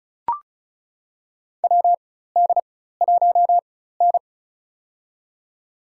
Morse code, heard as a keyed tone, about 36 wpm, spaced as for 16 wpm, 700 Hz. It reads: WD1N
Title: USA Call Signs - 4 Characters - 4x Character Spacing - 35wpm